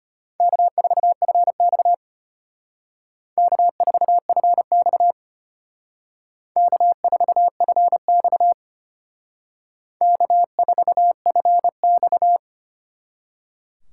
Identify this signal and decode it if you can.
Morse code, heard as a keyed tone, about 30 wpm, 700 Hz. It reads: K4FX K4FX K4FX K4FX
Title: Speed Racing - USA 1x2 Call Signs 25wpm